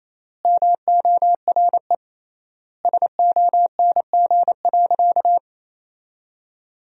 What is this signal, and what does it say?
Morse code, heard as a keyed tone, about 28 wpm, 700 Hz. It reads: MORE SONG.
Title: 3x Word-Spacing - Sentences from Top 400 Words - 28wpm